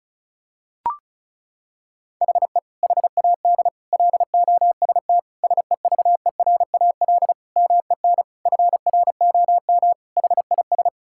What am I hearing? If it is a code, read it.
Morse code, heard as a keyed tone, about 36 wpm, 700 Hz. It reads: HE HAD LOST SEVERAL MEN FROM HIS
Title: Sentences from Top 1600 Words 35wpm